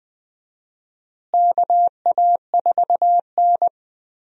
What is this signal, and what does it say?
Morse code, heard as a keyed tone, about 20 wpm, 700 Hz. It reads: KA4N
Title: USA Call Signs - 4 Characters - Repeated 2x - 20wpm